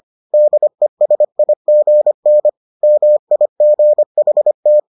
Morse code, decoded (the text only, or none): DESIGN MIGHT